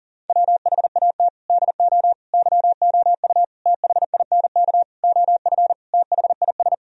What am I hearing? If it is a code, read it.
Morse code, heard as a keyed tone, about 40 wpm, 700 Hz. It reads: WHAT DO YOU THINK OF THIS
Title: Sentences from Top 1000 Words - Review Part 2 - 40wpm